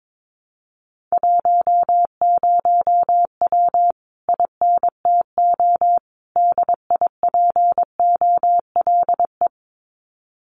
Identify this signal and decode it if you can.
Morse code, heard as a keyed tone, about 22 wpm, 700 Hz. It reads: 10W INTO DIPOLE
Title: Common QSO Phrases Encore 22wpm